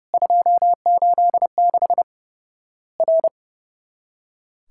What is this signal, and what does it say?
Morse code, heard as a keyed tone, about 30 wpm, 700 Hz. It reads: 286 R
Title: Worked All Europe DX Contest - QTC Exchange - 30wpm